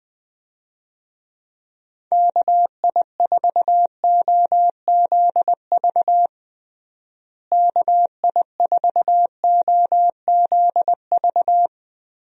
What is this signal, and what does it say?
Morse code, heard as a keyed tone, about 21 wpm, 700 Hz. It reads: KI4OZV KI4OZV
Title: USA Call Signs - 6 Characters - Repeated 3x - 20wpm